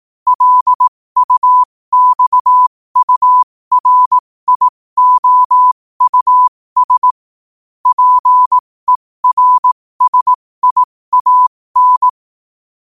Morse code, heard as a keyed tone, about 18 wpm, spaced as for 15 wpm, 1000 Hz. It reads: LUXURIOUS PERSIAN
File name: qotdfiles / 15wpm